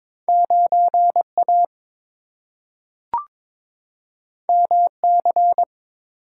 Morse code, heard as a keyed tone, about 22 wpm, 700 Hz. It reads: 9A MC
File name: practice / 22wpm / Two Letter-Number Combinations - 22wpm